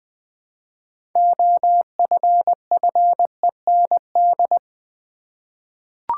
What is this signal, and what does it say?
Morse code, heard as a keyed tone, about 20 wpm, 700 Hz. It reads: OFFEND
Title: 6 Letter Words - First 3 Letters Given - 20wpm